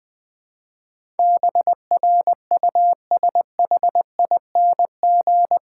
Morse code, heard as a keyed tone, about 20 wpm, 700 Hz. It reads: BRUSHING
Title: Difficult Sentences with Hints - 20wpm